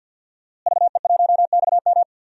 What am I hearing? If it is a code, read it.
Morse code, heard as a keyed tone, about 48 wpm, 700 Hz. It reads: VE1XM
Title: DXCC-100 Call Signs - 5 Characters - Repeated 2x - 50wpm